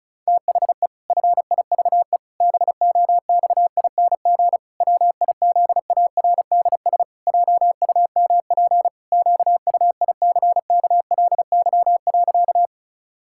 Morse code, heard as a keyed tone, about 36 wpm, 700 Hz. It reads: THE FIVE BOXING WIZARDS JUMP QUICKLY.
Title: Pangram Challenge v2 - 35wpm